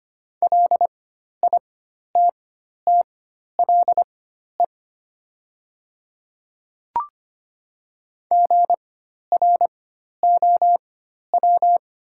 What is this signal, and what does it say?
Morse code, heard as a keyed tone, about 25 wpm, spaced as for 12 wpm, 690 Hz. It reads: LITTLE GROW
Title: 4x Character Spacing - Top 300 Words - 25wpm